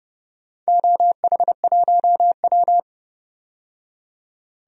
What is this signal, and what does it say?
Morse code, heard as a keyed tone, about 30 wpm, 700 Hz. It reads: OH1W
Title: DXCC-100 Call Signs - 4 Characters - 30wpm